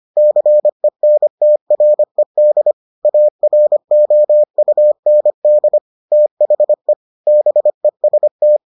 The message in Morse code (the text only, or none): CENTRED AROUND THE BEST